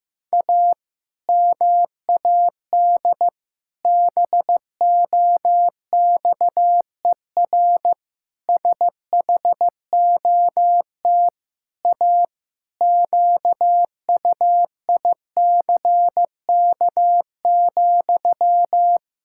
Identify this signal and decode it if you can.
Morse code, heard as a keyed tone, about 15 wpm, 700 Hz. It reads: A MAD BOXER SHOT A QUICK,
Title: Pangram Challenge v2 - 15wpm